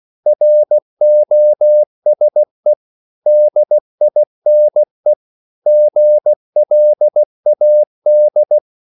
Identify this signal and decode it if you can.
Morse code, heard as a keyed tone, about 16 wpm, 600 Hz. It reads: ROSE DINE GLAD
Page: Korte danske ord 16wpm